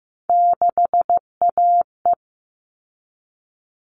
Morse code, heard as a keyed tone, about 15 wpm, 700 Hz. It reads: 6AE
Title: All Cut Numbers - 3-Digit 15wpm